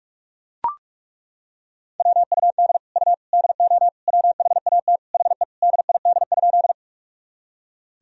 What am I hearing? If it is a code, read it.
Morse code, heard as a keyed tone, about 41 wpm, 700 Hz. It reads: WUD U DO WHAT HE DID?
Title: Sentences from Top 100 Words and Abbreviations 45wpm